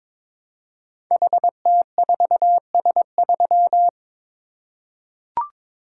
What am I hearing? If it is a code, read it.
Morse code, heard as a keyed tone, about 22 wpm, 700 Hz. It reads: HT4S3
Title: Code Groups - 5 Letter:Numbers - 22wpm